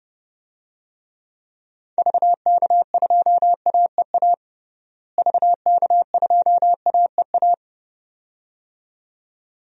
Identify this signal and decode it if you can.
Morse code, heard as a keyed tone, about 29 wpm, 700 Hz. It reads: VK2AEA VK2AEA
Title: DXCC-100 Call Signs - 6 Characters - Repeated 2x - 30wpm